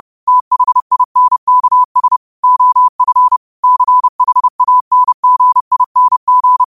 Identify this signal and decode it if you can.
Morse code, heard as a keyed tone, about 30 wpm, 1000 Hz. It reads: THINKS OF CHANGING
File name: qotdfiles / 30wpm